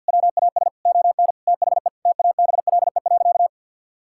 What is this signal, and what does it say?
Morse code, heard as a keyed tone, about 52 wpm, 700 Hz. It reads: WAS ON THE TABLE.
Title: Sentences from Top 1000 Words - Review Part 6 - 50wpm